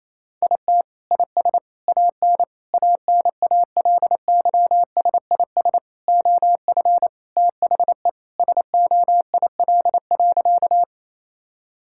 Morse code, heard as a keyed tone, about 28 wpm, 700 Hz. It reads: IT IS AN ANALYSIS OF THE SOIL.
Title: Sentences from Top 1000 Words - Review Part 5 - 28wpm